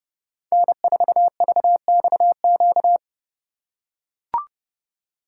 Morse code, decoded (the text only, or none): N4VXQ